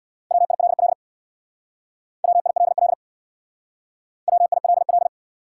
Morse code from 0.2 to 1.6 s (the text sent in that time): WILL